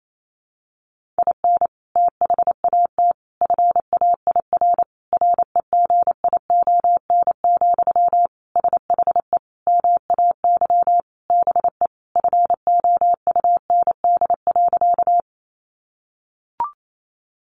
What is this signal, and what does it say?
Morse code, heard as a keyed tone, about 28 wpm, 700 Hz. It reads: IN THAT FAIR REGION, SHE MAY BE FOUND.
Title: Sentences from Top 1500 Words 28wpm